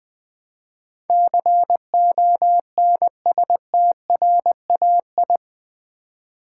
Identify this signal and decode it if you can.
Morse code, heard as a keyed tone, about 20 wpm, 700 Hz. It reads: CONSTRAI
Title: Build A Word - 4 Letters At A Time - 10-Letter Words - 20wpm